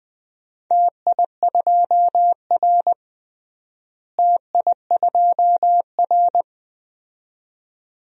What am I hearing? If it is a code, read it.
Morse code, heard as a keyed tone, about 19 wpm, 700 Hz. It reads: TI2R TI2R
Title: DXCC-100 Call Signs - 4 Characters - Repeated 2x - 20wpm